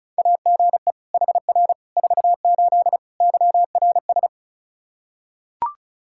Morse code, heard as a keyed tone, about 36 wpm, 700 Hz. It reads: AGE HR 48 YRS
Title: Practice QSO - Exchange 3 - More Info - 35wpm